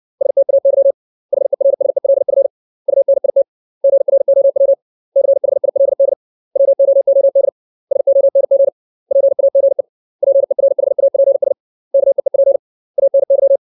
Morse code, heard as a keyed tone, about 60 wpm, 550 Hz. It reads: STAY HERSELF UNIT GROW CHILD WOOD SONG PAGE PERHAPS KEEP ANY
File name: 632 words / 632 Words 60WPM